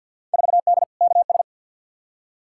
Morse code, heard as a keyed tone, about 46 wpm, 700 Hz. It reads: 4D KS
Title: Field Day Exchanges Encore - 50wpm